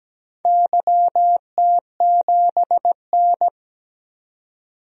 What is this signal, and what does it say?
Morse code, heard as a keyed tone, about 17 wpm, 700 Hz. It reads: YT7N
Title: DXCC-100 Call Signs - 4 Characters - 17wpm